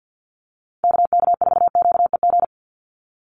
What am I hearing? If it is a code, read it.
Morse code, heard as a keyed tone, about 50 wpm, 700 Hz. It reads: KK4QEG
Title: USA Call Signs 50wpm